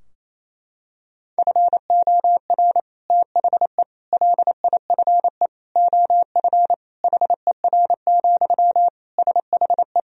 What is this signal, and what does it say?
Morse code, heard as a keyed tone, about 28 wpm, 700 Hz. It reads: FOR THE LIFE OF HER, SHE
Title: Sentences from Top 1000 Words - Review Part 2 - 28wpm